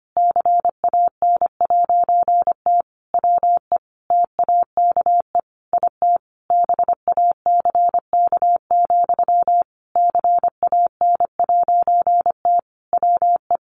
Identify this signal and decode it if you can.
Morse code, heard as a keyed tone, about 25 wpm, 700 Hz. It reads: CAN'T WE TAKE IT BACK, CAN'T WE
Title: A Sound Of Thunder 25wpm